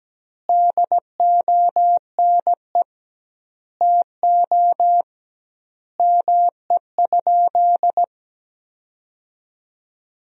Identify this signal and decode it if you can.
Morse code, heard as a keyed tone, about 17 wpm, 700 Hz. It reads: DONE TO ME?
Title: 2x Word-Spacing - Sentences from Top 500 Words - 17wpm